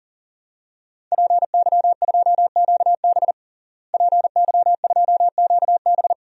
Morse code, heard as a keyed tone, about 40 wpm, 700 Hz. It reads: PY2QB PY2QB
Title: DXCC-100 Call Signs - 5 Characters - Repeated 2x - 40wpm